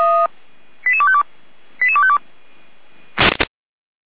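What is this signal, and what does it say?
Five-tone selective call (ZVEI), heard as quick runs of five tones, tones of about 70 ms. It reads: 88252 88252
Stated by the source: ZVEI five-tone call received on SDR by feuerwehr-orlamuende.org (fire brigade Orlamünde).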